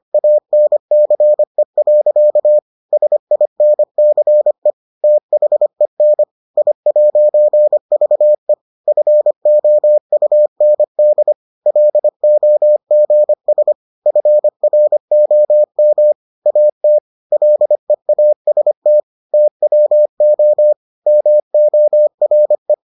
Morse code, heard as a keyed tone, about 25 wpm, 600 Hz. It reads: ANCE. SINCE THEN I'VE FOUND LOGS FROM AT LEAST TWO MORE